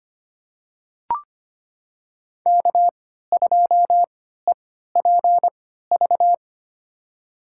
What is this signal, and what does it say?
Morse code, heard as a keyed tone, about 25 wpm, spaced as for 14 wpm, 700 Hz. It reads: K2EPV
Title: USA Call Signs - 5 Characters - 3x Character Spacing - 25wpm